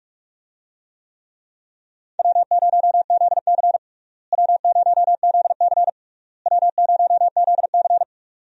Morse code, heard as a keyed tone, about 42 wpm, 700 Hz. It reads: W0ZC W0ZC W0ZC
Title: USA Call Signs - 4 Characters - Repeated 3x - 45wpm